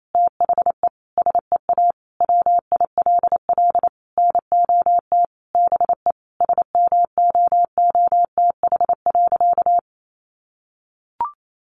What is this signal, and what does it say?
Morse code, heard as a keyed tone, about 28 wpm, 700 Hz. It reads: THE SEA WILL NOT BE SMOOTH.
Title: Sentences from Top 1400 Words 28wpm